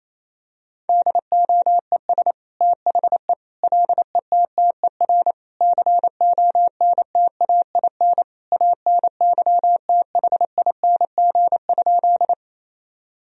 Morse code, heard as a keyed tone, about 28 wpm, 700 Hz. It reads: DOES THE LETTER CONTAIN ANYTHING?
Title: Sentences from Top 1000 Words - Review Part 2 - 28wpm